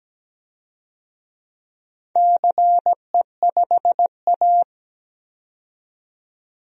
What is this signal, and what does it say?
Morse code, heard as a keyed tone, about 17 wpm, 700 Hz. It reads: CE5A
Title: DXCC-100 Call Signs - 4 Characters - 17wpm